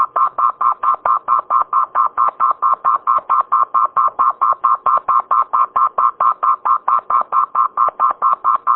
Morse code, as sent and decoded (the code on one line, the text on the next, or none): none